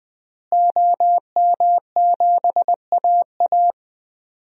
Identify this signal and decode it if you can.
Morse code, heard as a keyed tone, about 20 wpm, 700 Hz. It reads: OM7AA
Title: DXCC-100 Call Signs - 5 Characters - Repeated 3x - 20wpm